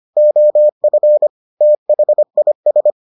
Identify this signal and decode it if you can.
Morse code, heard as a keyed tone, about 25 wpm, 600 Hz. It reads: OF THIS